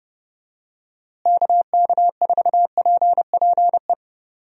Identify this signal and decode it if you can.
Morse code, heard as a keyed tone, about 30 wpm, 700 Hz. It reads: KK4PPE
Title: USA Call Signs - 6 Characters - Repeated 3x - 30wpm